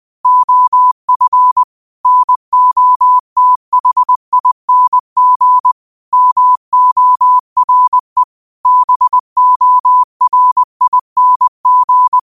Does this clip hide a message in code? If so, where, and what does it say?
Yes: Morse code from 0.3 to 12.2 s, heard as a keyed tone, about 20 wpm, 1000 Hz. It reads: OF NOTHING MORE BORING